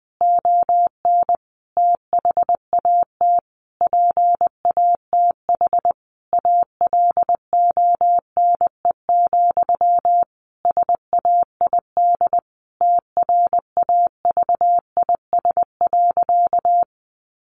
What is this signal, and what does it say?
Morse code, heard as a keyed tone, about 20 wpm, 700 Hz. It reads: ON THAT PATH ALONE, SAID TRAVIS.
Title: A Sound Of Thunder 20wpm